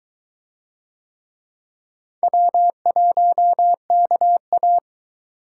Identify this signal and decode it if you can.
Morse code, heard as a keyed tone, about 21 wpm, 700 Hz. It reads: W1KA